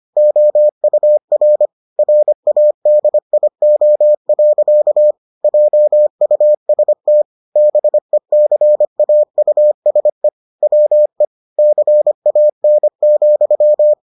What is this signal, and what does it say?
Morse code, heard as a keyed tone, about 25 wpm, 600 Hz. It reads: OUR RADIO. JUST BECAUSE WE CAN,